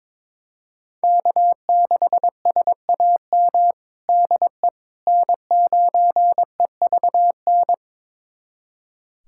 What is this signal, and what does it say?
Morse code, heard as a keyed tone, about 21 wpm, 700 Hz. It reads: K6SAM DE N9EVN
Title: USA Callsign DE Callsign - 5 Characters - 22wpm